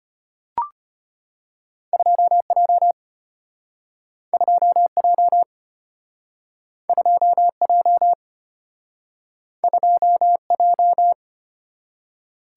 Morse code, heard as a keyed tone, about 34 wpm, 700 Hz. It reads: 2J 2J 2J 2J